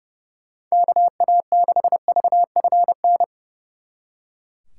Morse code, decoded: KA6VFN

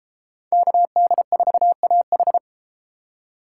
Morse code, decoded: KD4AH

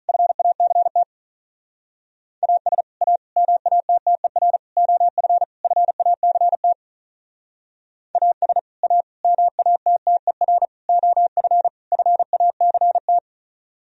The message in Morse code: FACT AS A MATTER OF FACT AS A MATTER OF FACT

